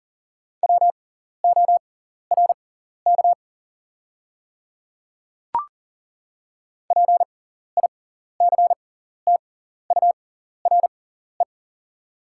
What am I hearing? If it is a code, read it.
Morse code, heard as a keyed tone, about 40 wpm, spaced as for 14 wpm, 700 Hz. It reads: WORK PICTURE
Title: 6x Character Spacing - Top 200 Words - 40wpm